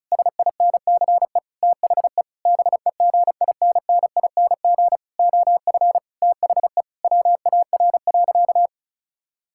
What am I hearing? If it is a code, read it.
Morse code, heard as a keyed tone, about 36 wpm, 700 Hz. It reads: SINCE THE BEGINNING OF THE WAR.
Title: Sentences from Top 1000 Words - Review Part 6 - 35wpm